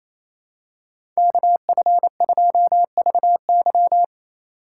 Morse code, heard as a keyed tone, about 28 wpm, 700 Hz. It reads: KF2VY
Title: USA Call Signs - 5 Characters - 2x Character Spacing - 28wpm